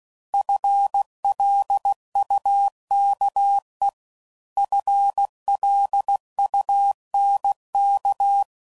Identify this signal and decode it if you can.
Morse code, heard as a keyed tone, about 16 wpm, 800 Hz. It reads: FLUKE FLUNK
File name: Prefixes / FLs & THs / FL Runs